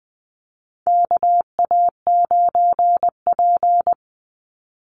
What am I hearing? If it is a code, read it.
Morse code, heard as a keyed tone, about 20 wpm, 700 Hz. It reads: KA9P
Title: Call Signs Repeated 2x - 20wpm